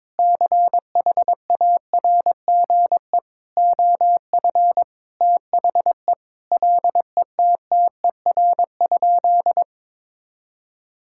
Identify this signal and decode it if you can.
Morse code, heard as a keyed tone, about 22 wpm, 700 Hz. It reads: CHARGE OF THE LETTER?
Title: Sentences from Top 1000 Words - Review Part 6 - 22wpm